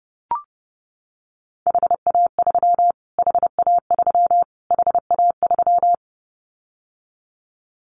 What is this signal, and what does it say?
Morse code, heard as a keyed tone, about 30 wpm, 700 Hz. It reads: HA3 HA3 HA3
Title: Callsign Prefixes for Top 25 DXCC - Repeated 3x - 30wpm